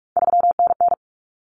Morse code, heard as a keyed tone, about 44 wpm, 700 Hz. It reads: 3NN
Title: Common Cut Numbers - 3-Digit 45wpm